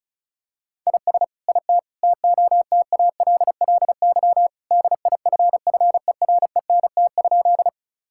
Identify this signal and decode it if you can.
Morse code, heard as a keyed tone, about 34 wpm, 700 Hz. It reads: IS IT TOTALLY DIFFERENT?
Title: Sentences from Top 1600 Words 35wpm